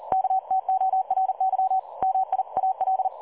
Morse code, heard as a keyed tone, about 38 wpm, 760 Hz. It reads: KTORY MIAL
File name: dlazaawansowanych / nagraniazpasma / websdr recording 3565.0kHz